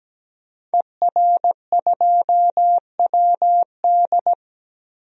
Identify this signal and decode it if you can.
Morse code, heard as a keyed tone, about 17 wpm, 700 Hz. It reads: ER2WD
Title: DXCC-100 Call Signs - 5 Characters - Repeated 3x - 17wpm